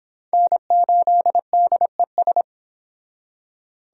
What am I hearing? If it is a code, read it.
Morse code, heard as a keyed tone, about 25 wpm, 700 Hz. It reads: N8DES